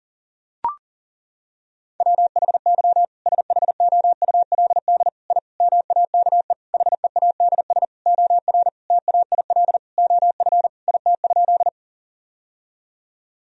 Morse code, heard as a keyed tone, about 40 wpm, 700 Hz. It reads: WHY SHOULD I MAKE HEADS OR TAIL OF IT?
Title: Sentences from Top 1000 Words - Review Part 3 - 40wpm